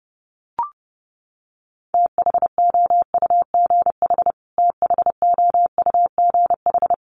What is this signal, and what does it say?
Morse code, heard as a keyed tone, about 30 wpm, 700 Hz. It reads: THOUGH THOUGH
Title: Top 500 Words Repeated 30wpm